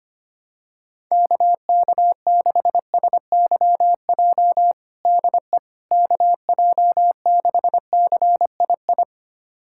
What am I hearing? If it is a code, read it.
Morse code, heard as a keyed tone, about 25 wpm, 700 Hz. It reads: KK6SYJ DE KJ6CII
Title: USA Callsign DE Callsign - 6 Characters - 25wpm